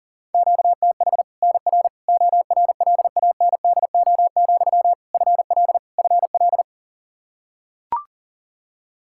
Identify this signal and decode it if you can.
Morse code, heard as a keyed tone, about 40 wpm, 700 Hz. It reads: QTH NR ORLANDO, FL FL
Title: Practice QSO - Exchange 1 - Beginning - 40wpm